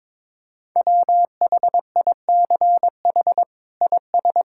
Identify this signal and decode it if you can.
Morse code, heard as a keyed tone, about 22 wpm, 700 Hz. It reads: WHICH IS
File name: practice / 22wpm / Sets of 8 Words - 22wpm